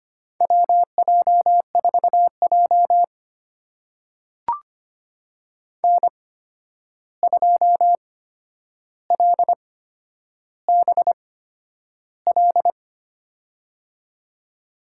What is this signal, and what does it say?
Morse code, heard as a keyed tone, about 25 wpm, spaced as for 10 wpm, 700 Hz. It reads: WJ4J N2LBL
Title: USA Call Signs - 8x Character Spacing - 25wpm